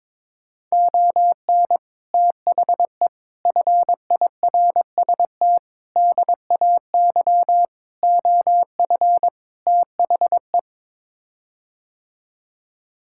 Morse code, heard as a keyed tone, about 22 wpm, 700 Hz. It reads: ON THE FIRST DAY OF THE
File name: practice / 22wpm / Sets of 6 Words - 22wpm